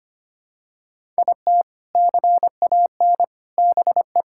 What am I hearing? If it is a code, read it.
Morse code, heard as a keyed tone, about 25 wpm, 700 Hz. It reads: IT CAN BE
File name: practice / 25wpm / Sentences with 4-Word Phrases - 25wpm